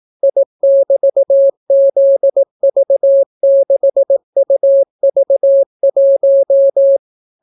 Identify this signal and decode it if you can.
Morse code, heard as a keyed tone, about 18 wpm, 550 Hz. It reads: I<BT>ZV6UV1